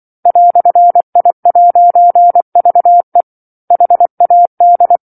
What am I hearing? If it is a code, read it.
Morse code, heard as a keyed tone, about 24 wpm, 700 Hz. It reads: "I'VE HAD
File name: 07-Chapter-07 24 WPM